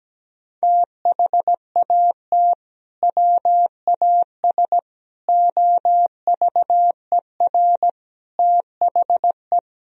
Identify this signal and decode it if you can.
Morse code, heard as a keyed tone, about 17 wpm, 700 Hz. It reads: THAT WAS OVER THE
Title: Sentences from Top 1000 Words - Review Part 2 - 17wpm